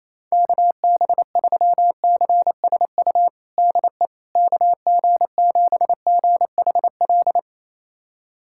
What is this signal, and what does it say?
Morse code, heard as a keyed tone, about 28 wpm, 700 Hz. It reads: KB3CSU DE KG7GHL
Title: USA Callsign DE Callsign - 6 Characters - 28wpm